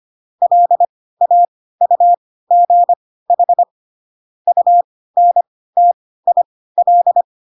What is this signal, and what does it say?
Morse code, heard as a keyed tone, about 25 wpm, spaced as for 16 wpm, 700 Hz. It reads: LAUGH UNTIL
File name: Pinocchio / Pinocchio ch3 10 25 ik1hge